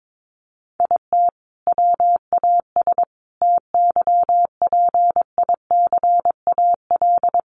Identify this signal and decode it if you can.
Morse code, heard as a keyed tone, about 22 wpm, 700 Hz. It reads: IT WAS TYPICAL
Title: Sentences from Top 1100 Words 22wpm